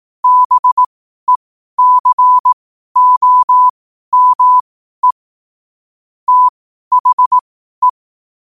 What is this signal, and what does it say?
Morse code, heard as a keyed tone, about 18 wpm, spaced as for 12 wpm, 1000 Hz. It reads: BECOME THE
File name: qotdfiles / 12wpm